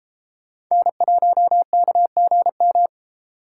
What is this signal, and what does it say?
Morse code, heard as a keyed tone, about 36 wpm, 700 Hz. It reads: N1KGM